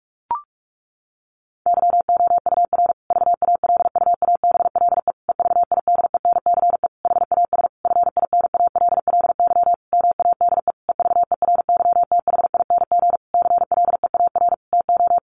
Morse code, heard as a keyed tone, about 44 wpm, 700 Hz. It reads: YOUR VALUABLE EVIDENCE HAS FINALLY MADE EVERYTHING CLEAR TO
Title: Sentences from Top 1300 Words 45wpm